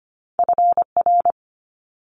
Morse code, heard as a keyed tone, about 24 wpm, 700 Hz. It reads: FR
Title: Multi-letter Phonemes Quick Fire 25wpm